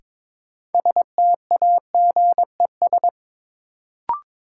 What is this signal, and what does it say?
Morse code, heard as a keyed tone, about 22 wpm, 700 Hz. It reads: STAGES
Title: Sets of 5 Words Encore - New Word Drill - 22wpm